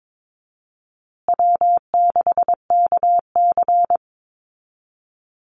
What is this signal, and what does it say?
Morse code, heard as a keyed tone, about 22 wpm, 700 Hz. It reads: W6KC
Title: Easy Call Signs Repeated 3x - 22wpm